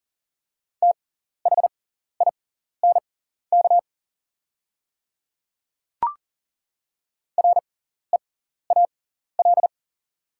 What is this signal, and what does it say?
Morse code, heard as a keyed tone, about 38 wpm, spaced as for 14 wpm, 700 Hz. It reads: THINK REAL
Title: 6x Character Spacing - Top 300 Words - 40wpm